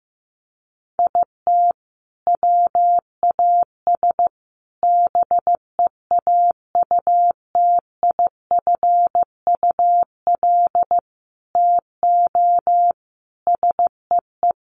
Morse code, heard as a keyed tone, about 15 wpm, 700 Hz. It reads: IT WAS BEAUTIFUL TO SEE
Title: Sentences from Top 900 Words 15wpm